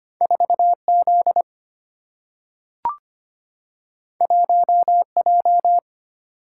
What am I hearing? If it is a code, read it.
Morse code, heard as a keyed tone, about 25 wpm, 690 Hz. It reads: 4Z 1J
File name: Two Letter-Number Combinations - 25wpm